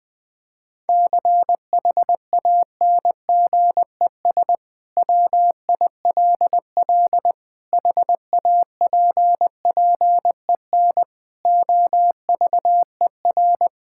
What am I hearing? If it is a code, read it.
Morse code, heard as a keyed tone, about 21 wpm, 700 Hz. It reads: CHANGES WILL HAPPEN OVER
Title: Sentences with 5-Word Phrases - Spoken Beginning - 20wpm